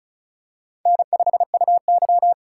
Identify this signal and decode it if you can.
Morse code, heard as a keyed tone, about 36 wpm, 700 Hz. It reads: N5UY
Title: USA Call Signs - 4 Characters - 2x Character Spacing - 35wpm